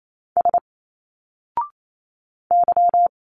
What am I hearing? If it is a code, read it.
Morse code, heard as a keyed tone, about 28 wpm, 700 Hz. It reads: S Y
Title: Single Letters 28wpm